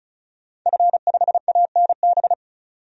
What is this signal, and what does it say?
Morse code, heard as a keyed tone, about 35 wpm, 700 Hz. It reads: F5ANB
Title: DXCC-100 Call Signs - 5 Characters - 35wpm